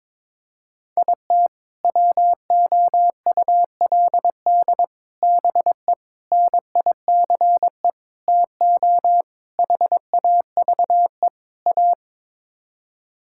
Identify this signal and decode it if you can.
Morse code, heard as a keyed tone, about 22 wpm, 700 Hz. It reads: IT WOULD BE NICE TO HAVE A
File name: practice / 22wpm / Sets of 7 Words - 22wpm